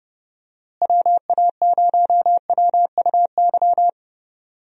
Morse code, heard as a keyed tone, about 30 wpm, 700 Hz. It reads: WA0WUY
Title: USA Call Signs - 6 Characters - Repeated 3x - 30wpm